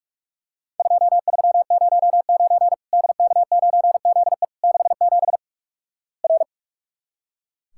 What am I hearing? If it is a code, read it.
Morse code, heard as a keyed tone, about 45 wpm, 700 Hz. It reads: 2309 DK9ZE 67 R
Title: Worked All Europe DX Contest - QTC Exchange - 45wpm